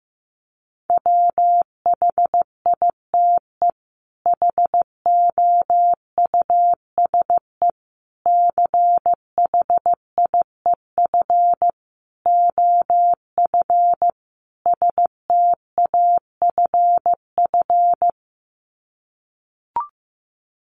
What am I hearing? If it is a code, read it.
Morse code, heard as a keyed tone, about 15 wpm, 700 Hz. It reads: WHITE HOUSE CHIEF OF STAFF
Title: Sets of 5 Words Encore 15wpm